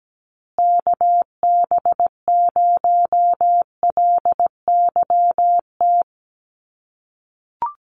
Morse code, heard as a keyed tone, about 17 wpm, 700 Hz. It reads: KB0LYT
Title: USA Call Signs Repeated 3x 17wpm